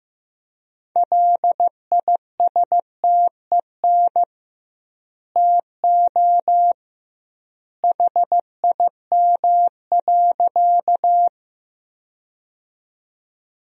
Morse code, heard as a keyed tone, about 15 wpm, 700 Hz. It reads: LISTEN TO HIM.
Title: 2x Word-Spacing - Sentences from Top 400 Words - 15wpm